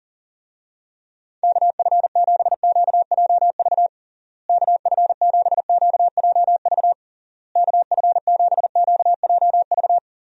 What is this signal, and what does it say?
Morse code, heard as a keyed tone, about 40 wpm, 700 Hz. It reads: KF7QJV KF7QJV KF7QJV
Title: USA Call Signs - 6 Characters - Repeated 3x - 40wpm